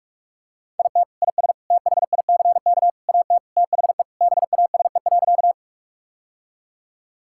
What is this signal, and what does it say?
Morse code, heard as a keyed tone, about 42 wpm, 700 Hz. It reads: IT IS THICK AT THE BASE.
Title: Sentences from Top 1600 Words 45wpm